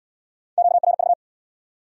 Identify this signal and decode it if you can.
Morse code, heard as a keyed tone, about 74 wpm, 700 Hz. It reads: QRV